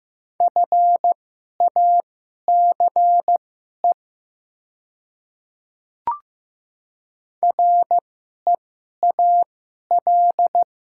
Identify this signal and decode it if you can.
Morse code, heard as a keyed tone, about 15 wpm, spaced as for 11 wpm, 700 Hz. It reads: FACE REAL
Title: 2x Character Spacing - Top 300 Words - 15wpm